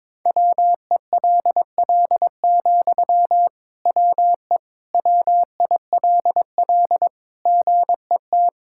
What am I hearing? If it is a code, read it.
Morse code, heard as a keyed tone, about 22 wpm, 700 Hz. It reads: WELL, WE WILL GET
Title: Sentences from Top 1000 Words - Review Part 6 - 22wpm